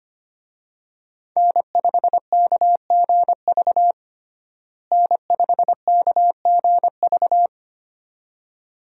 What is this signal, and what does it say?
Morse code, heard as a keyed tone, about 25 wpm, 700 Hz. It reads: N5KGV N5KGV
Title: USA Call Signs - 5 Characters - Repeated 2x - 25wpm